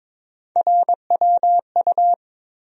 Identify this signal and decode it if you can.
Morse code, heard as a keyed tone, about 22 wpm, 700 Hz. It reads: RWU
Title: USA Call Sign Postfixes - 3 Letters 22wpm